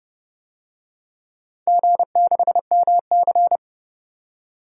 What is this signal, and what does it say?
Morse code, heard as a keyed tone, about 30 wpm, 700 Hz. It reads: G6MC
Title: DXCC-100 Call Signs - 4 Characters - 30wpm